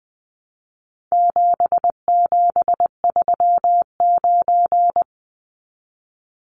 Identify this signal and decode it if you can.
Morse code, heard as a keyed tone, about 20 wpm, 700 Hz. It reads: 7739
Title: Numbers - 4-Digit 20wpm